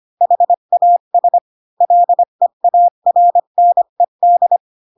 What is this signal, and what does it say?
Morse code, heard as a keyed tone, about 25 wpm, 700 Hz. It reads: HAS LEARNED
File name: Pinocchio / Pinocchio ch1 20 25 ik1hge